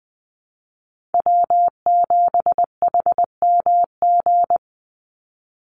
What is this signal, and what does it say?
Morse code, heard as a keyed tone, about 19 wpm, 700 Hz. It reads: W7HMG
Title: USA Call Signs 20wpm